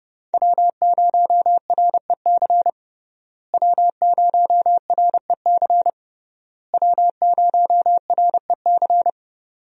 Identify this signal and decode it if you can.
Morse code, heard as a keyed tone, about 30 wpm, 700 Hz. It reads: W0REC W0REC W0REC
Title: USA Call Signs - 5 Characters - Repeated 3x - 30wpm